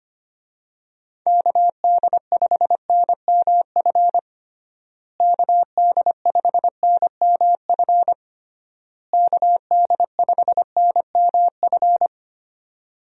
Morse code, heard as a keyed tone, about 25 wpm, 700 Hz. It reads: KD5NMF KD5NMF KD5NMF
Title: USA Call Signs - 6 Characters - Repeated 3x - 25wpm